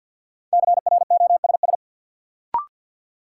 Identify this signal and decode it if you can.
Morse code, heard as a keyed tone, about 49 wpm, 700 Hz. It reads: CROSS
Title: Sentences from Top 300 Words - New Word Drill - 50wpm